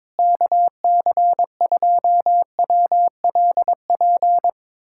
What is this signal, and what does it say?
Morse code, heard as a keyed tone, about 21 wpm, 700 Hz. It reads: KC2WLP
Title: USA Callsign DE Callsign 22wpm